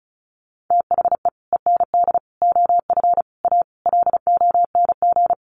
Morse code, heard as a keyed tone, about 34 wpm, 700 Hz. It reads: THE END OF A LONG
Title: Sets of 5 Words Encore2 35wpm